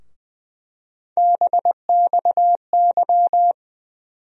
Morse code, heard as a keyed tone, about 20 wpm, 700 Hz. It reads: BXY
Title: USA Call Sign Postfixes - 3 Letters 20wpm